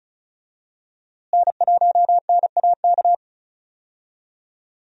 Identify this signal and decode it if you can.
Morse code, heard as a keyed tone, about 36 wpm, 700 Hz. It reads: N1NAK
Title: USA Call Signs - 5 Characters - Repeated 2x - 35wpm